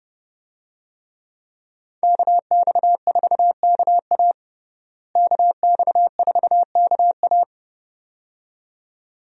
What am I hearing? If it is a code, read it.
Morse code, heard as a keyed tone, about 30 wpm, 700 Hz. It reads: KX4KA KX4KA
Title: USA Call Signs - 5 Characters - Repeated 2x - 30wpm